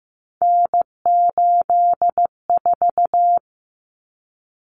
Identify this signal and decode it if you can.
Morse code, heard as a keyed tone, about 15 wpm, 700 Hz. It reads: N84
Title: Common Cut Numbers - 3-Digit 15wpm